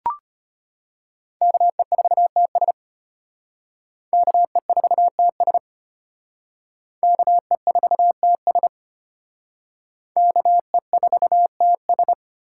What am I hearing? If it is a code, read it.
Morse code, heard as a keyed tone, about 32 wpm, 700 Hz. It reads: KE4TS KE4TS KE4TS KE4TS